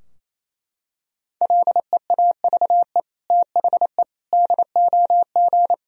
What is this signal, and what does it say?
Morse code, heard as a keyed tone, about 28 wpm, 690 Hz. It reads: LEAVE THE DOG
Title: Sentences from Top 1000 Words - Review Part 2 - 28wpm